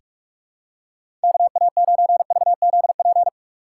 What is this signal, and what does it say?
Morse code, heard as a keyed tone, about 46 wpm, 700 Hz. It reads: KA9VZP
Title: USA Call Signs - 6 Characters - 2x Character Spacing - 45wpm